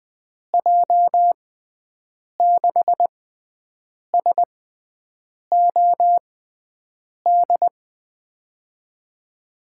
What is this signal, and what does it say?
Morse code, heard as a keyed tone, about 20 wpm, spaced as for 7 wpm, 700 Hz. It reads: J6SOD
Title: USA Call Signs - 6x Character Spacing - 20wpm